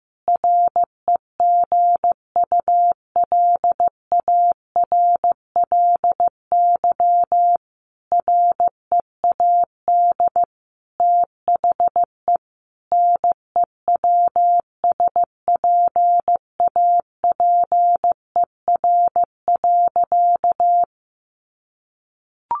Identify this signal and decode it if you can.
Morse code, heard as a keyed tone, about 15 wpm, 700 Hz. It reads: REGULARLY READ THE NEWSPAPER.
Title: Sentences from Top 1400 Words 15wpm